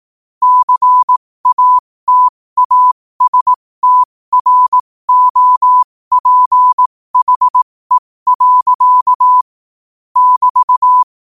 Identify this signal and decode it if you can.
Morse code, heard as a keyed tone, about 18 wpm, spaced as for 15 wpm, 1000 Hz. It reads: CATASTROPHE. <BT>